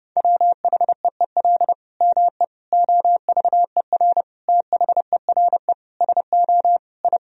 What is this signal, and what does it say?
Morse code, heard as a keyed tone, about 30 wpm, 700 Hz. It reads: WHEEL ME OVER THERE SO I
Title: Sentences from Top 1000 Words - Review Part 3 - 30wpm